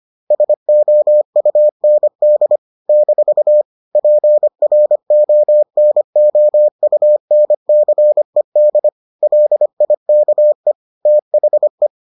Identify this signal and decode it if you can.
Morse code, heard as a keyed tone, about 25 wpm, 600 Hz. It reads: SOUND - PRONOUNCED LIKE THE